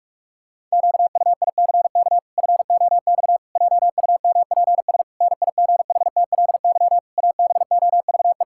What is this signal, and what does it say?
Morse code, heard as a keyed tone, about 45 wpm, 700 Hz. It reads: QUICK FOX JUMPS NIGHTLY ABOVE
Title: Pangram Challenge v2 - 45wpm